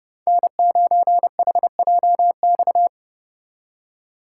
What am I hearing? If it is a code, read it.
Morse code, heard as a keyed tone, about 30 wpm, 700 Hz. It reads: N9HJX